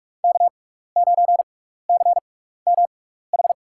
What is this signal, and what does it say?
Morse code, heard as a keyed tone, about 44 wpm, spaced as for 15 wpm, 700 Hz. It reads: K9CMH
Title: USA Call Signs - 5 Characters - 6x Character Spacing - 45wpm